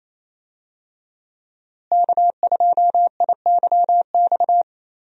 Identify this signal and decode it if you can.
Morse code, heard as a keyed tone, about 28 wpm, 700 Hz. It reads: K2IYX
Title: USA Call Signs - 5 Characters - Repeated 2x - 28wpm